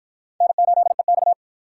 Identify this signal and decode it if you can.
Morse code, heard as a keyed tone, about 53 wpm, 700 Hz. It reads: N8EX